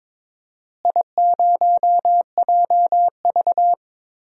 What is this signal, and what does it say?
Morse code, heard as a keyed tone, about 21 wpm, 700 Hz. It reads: I0JV
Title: DXCC-100 Call Signs - 4 Characters - Repeated 2x - 22wpm